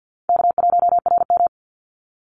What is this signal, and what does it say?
Morse code, heard as a keyed tone, about 50 wpm, 700 Hz. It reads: K1RM
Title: Easy Call Signs - 50wpm